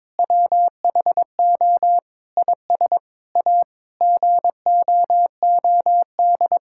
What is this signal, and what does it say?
Morse code, heard as a keyed tone, about 23 wpm, 700 Hz. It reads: WHO IS A GOOD
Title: Sentences from Top 1000 Words - Review Part 5 - 22wpm